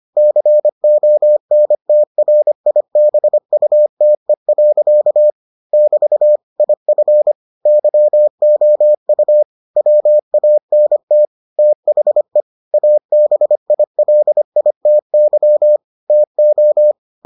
Morse code, heard as a keyed tone, about 25 wpm, 600 Hz. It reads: CONTRIBUTE. <BT> IF YOU WANT THE ABILITY TO